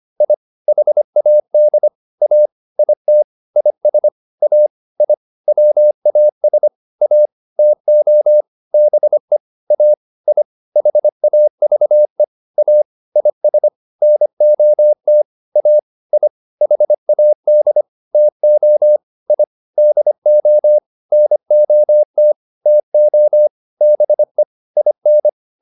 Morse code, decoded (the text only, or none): I HAD A IT IS A I WAS A TO BE A I HAVE A IS NOT A I HAD TO I DO NOT TO BE IN